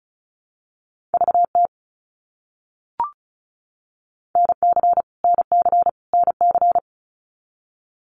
Morse code, heard as a keyed tone, about 34 wpm, 700 Hz. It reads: VT NC NC NC
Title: US States Repeated 3x 35wpm